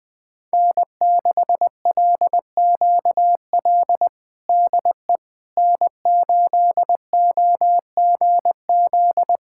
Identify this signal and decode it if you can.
Morse code, heard as a keyed tone, about 21 wpm, 700 Hz. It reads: N6LQL DE N8OGZ
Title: USA Callsign DE Callsign - 5 Characters - 20wpm